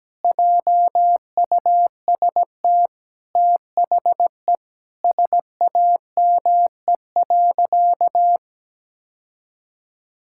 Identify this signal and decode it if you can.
Morse code, heard as a keyed tone, about 17 wpm, 700 Hz. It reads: JUST THE SAME.
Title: Sentences from Top 1600 Words 17wpm